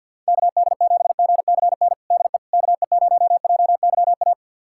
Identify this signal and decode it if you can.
Morse code, heard as a keyed tone, about 47 wpm, 700 Hz. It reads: KD7GCN DE KE0JYA
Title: USA Callsign DE Callsign 50wpm